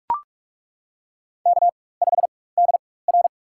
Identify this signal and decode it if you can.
Morse code, heard as a keyed tone, about 42 wpm, spaced as for 21 wpm, 700 Hz. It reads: K5DR
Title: USA Call Signs - 4 Characters - 4x Character Spacing - 45wpm